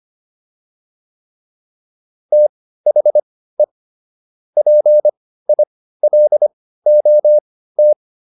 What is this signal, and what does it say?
Morse code, heard as a keyed tone, about 25 wpm, spaced as for 15 wpm, 600 Hz. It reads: THE PILOT